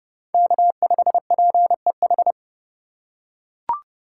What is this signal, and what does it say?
Morse code, heard as a keyed tone, about 30 wpm, 700 Hz. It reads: K5PEH
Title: USA Call Signs - 5 Characters - 3x Character Spacing - 30wpm